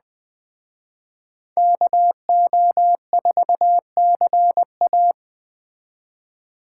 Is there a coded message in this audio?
Yes: Morse code from 1.6 to 5.1 s, heard as a keyed tone, about 20 wpm, 700 Hz. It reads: KO4CA